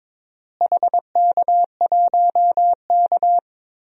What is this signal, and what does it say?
Morse code, heard as a keyed tone, about 22 wpm, 700 Hz. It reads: HK1K
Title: DXCC-100 Call Signs - 4 Characters - Repeated 2x - 22wpm